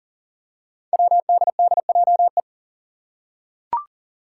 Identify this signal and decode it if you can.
Morse code, heard as a keyed tone, about 40 wpm, 700 Hz. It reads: WDDJE
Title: Five Challenging Letter-Number Combinations 40wpm